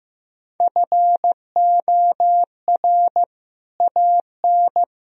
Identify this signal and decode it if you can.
Morse code, heard as a keyed tone, about 15 wpm, 700 Hz. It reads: FOR AN
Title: Sentences from Top 1000 Words - Review Part 4 - 15wpm